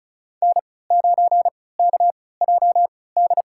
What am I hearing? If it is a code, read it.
Morse code, heard as a keyed tone, about 34 wpm, spaced as for 20 wpm, 700 Hz. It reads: N9KJD
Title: USA Call Signs - 5 Characters - 3x Character Spacing - 35wpm